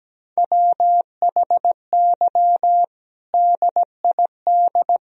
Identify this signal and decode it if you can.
Morse code, heard as a keyed tone, about 17 wpm, 700 Hz. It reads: WHY DID
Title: Sentences from Top 500 Words and Abbreviations 17wpm